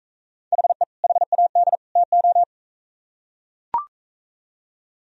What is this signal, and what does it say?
Morse code, heard as a keyed tone, about 42 wpm, 700 Hz. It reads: HE HAD TO